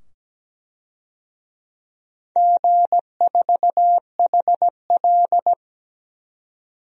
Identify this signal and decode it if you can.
Morse code, heard as a keyed tone, about 17 wpm, 700 Hz. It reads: G4HL
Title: DXCC-100 Call Signs - 4 Characters - 17wpm